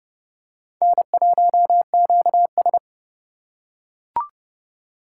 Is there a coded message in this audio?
Yes: Morse code from 0.8 to 2.8 s, heard as a keyed tone, about 29 wpm, 700 Hz. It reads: N1QS